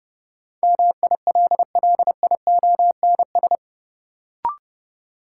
Morse code, heard as a keyed tone, about 30 wpm, 700 Hz. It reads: MILLIONS
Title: Sets of 5 Words - New Word Drill - 30wpm